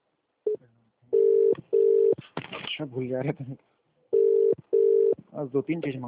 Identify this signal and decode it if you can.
Ringback tone (UK), 400 + 450 Hz, with speech.